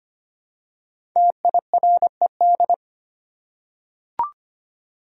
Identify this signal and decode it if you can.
Morse code, heard as a keyed tone, about 25 wpm, 700 Hz. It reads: TIRED
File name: 5 Letter Words - First 3 Letters Given - 25wpm